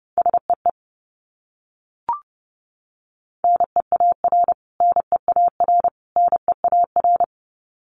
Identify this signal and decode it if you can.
Morse code, heard as a keyed tone, about 30 wpm, 700 Hz. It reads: SEE NEAR NEAR NEAR
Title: Top 300 Words Repeated 30wpm